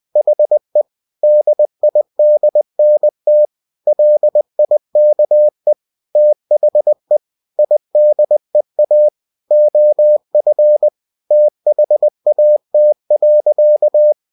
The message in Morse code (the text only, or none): HE DIDNT LIKE THE IDEA OF THAT.